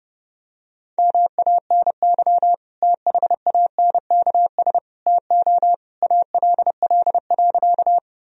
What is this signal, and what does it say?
Morse code, heard as a keyed tone, about 30 wpm, 700 Hz. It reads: MANY THANKS TO ALL.
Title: Sentences from Top 1000 Words - Review Part 3 - 30wpm